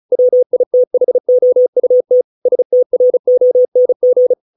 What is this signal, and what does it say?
Morse code, heard as a keyed tone, about 34 wpm, 500 Hz. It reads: WITHOUT STRONG
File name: 632 words / 632 Words 35WPM